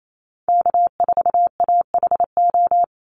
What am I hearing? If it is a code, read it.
Morse code, heard as a keyed tone, about 28 wpm, 700 Hz. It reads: K4AHO
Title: Call Signs Repeated 2x - 28wpm